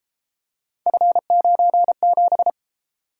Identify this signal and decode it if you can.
Morse code, heard as a keyed tone, about 33 wpm, 700 Hz. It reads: F97